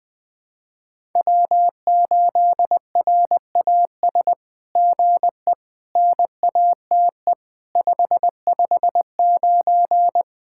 Morse code, heard as a keyed tone, about 20 wpm, 700 Hz. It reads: W8RAS GE NATE 559
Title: SOTA QSO Practice 20wpm